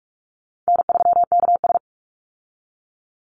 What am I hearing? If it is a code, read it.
Morse code, heard as a keyed tone, about 44 wpm, 700 Hz. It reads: N3KS
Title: Easy Call Signs Repeated 3x - 45wpm